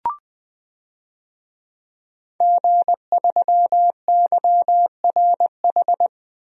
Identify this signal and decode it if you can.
Morse code, heard as a keyed tone, about 20 wpm, 700 Hz. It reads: G3YRH
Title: DXCC-100 Call Signs - 5 Characters - 20wpm